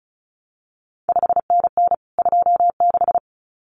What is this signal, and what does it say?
Morse code, heard as a keyed tone, about 36 wpm, 700 Hz. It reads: 5NN 26
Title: CQ WW DX Contest Exchange Encore - 35wpm